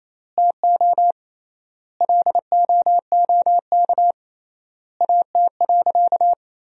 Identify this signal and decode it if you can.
Morse code, heard as a keyed tone, about 27 wpm, 700 Hz. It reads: TO LOOK AT.
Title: 3x Word-Spacing - Sentences from Top 200 Words - 28wpm